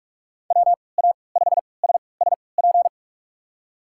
Morse code, heard as a keyed tone, about 46 wpm, spaced as for 26 wpm, 700 Hz. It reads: WA5SSP
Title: USA Call Signs - 3x Character Spacing - 45wpm